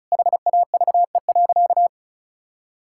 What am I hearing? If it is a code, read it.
Morse code, heard as a keyed tone, about 34 wpm, 700 Hz. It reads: HAVE.
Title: Sentences from Top 1000 Words - Review Part 6 - 35wpm